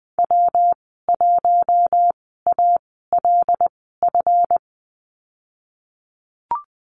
Morse code, heard as a keyed tone, about 20 wpm, spaced as for 15 wpm, 700 Hz. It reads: W1ALF
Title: 193 Lesson 31g - Letter v - Call Signs 20wpm